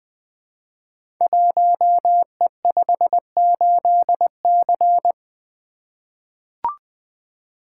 Morse code, heard as a keyed tone, about 20 wpm, 700 Hz. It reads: 1E58C
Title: Code Groups - 5 Letter:Numbers - 20wpm